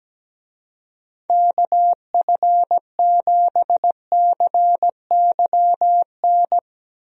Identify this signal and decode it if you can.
Morse code, heard as a keyed tone, about 17 wpm, 700 Hz. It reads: KF7CYN
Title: USA Call Signs - 6 Characters - 17wpm